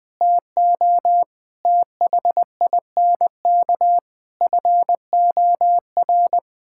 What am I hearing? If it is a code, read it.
Morse code, heard as a keyed tone, about 21 wpm, 700 Hz. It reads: TO THINK FOR
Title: Sentences from Top 1000 Words - Review Part 6 - 20wpm